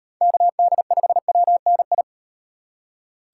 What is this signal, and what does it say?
Morse code, heard as a keyed tone, about 38 wpm, 700 Hz. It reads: KD5WNI